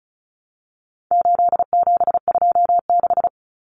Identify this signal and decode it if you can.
Morse code, heard as a keyed tone, about 34 wpm, 700 Hz. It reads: 8726
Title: Common Cut Numbers - 4-Digit 35wpm